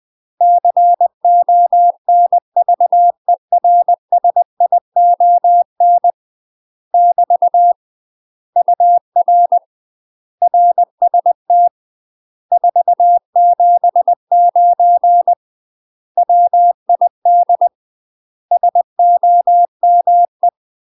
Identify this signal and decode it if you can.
Morse code, heard as a keyed tone, about 20 wpm, 700 Hz. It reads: CONVERSION <BT> UR RST 479 WID SOME
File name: UK MASTER QSOs TRAINING / 20WPM / UK MASTER QSOs 0046